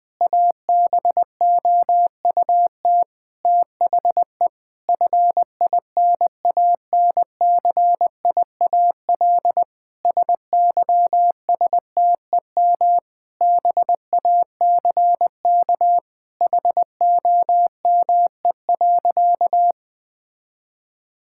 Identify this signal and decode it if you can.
Morse code, heard as a keyed tone, about 21 wpm, 700 Hz. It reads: ABOUT THE FINANCIAL SYSTEM BACK HOME.
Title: Sentences from Top 1000 Words - Review Part 5 - 20wpm